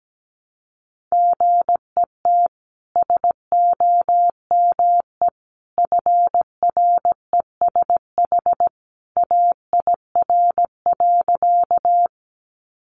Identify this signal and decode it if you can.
Morse code, heard as a keyed tone, about 17 wpm, 700 Hz. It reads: GET SOME FRESH AIR.
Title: Sentences from Top 1200 Words 17wpm